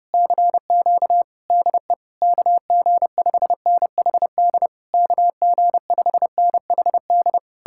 Morse code, heard as a keyed tone, about 33 wpm, 700 Hz. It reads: CQ DE KG5NHD KG5NHD